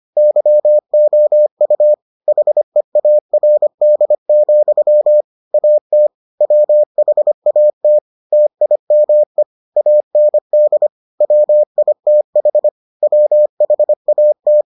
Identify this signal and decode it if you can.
Morse code, heard as a keyed tone, about 25 wpm, 600 Hz. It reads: YOU HEARD, AT WHAT TIME AND WITH WHAT